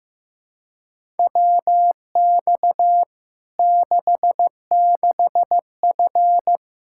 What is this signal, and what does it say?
Morse code, heard as a keyed tone, about 15 wpm, 700 Hz. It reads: WX 66F
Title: Practice QSO - Exchange 2 - Info - 15wpm